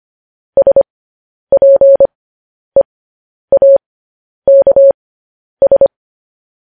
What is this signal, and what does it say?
Morse code, heard as a keyed tone, about 25 wpm, spaced as for 10 wpm, 550 Hz. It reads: SPEAKS
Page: news-headlines-in-morse-code-at-10-wpm